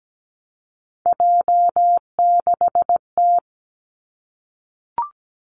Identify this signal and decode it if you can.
Morse code, heard as a keyed tone, about 17 wpm, 700 Hz. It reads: J6T
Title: Three Challenging Letter-Number Combinations 17wpm